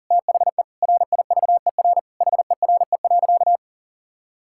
Morse code, heard as a keyed tone, about 40 wpm, 700 Hz. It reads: THE RIVER HERE.
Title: Sentences from Top 1000 Words - Review Part 5 - 40wpm